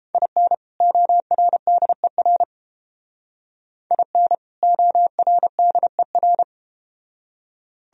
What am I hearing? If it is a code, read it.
Morse code, heard as a keyed tone, about 31 wpm, 700 Hz. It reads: IN ORDER IN ORDER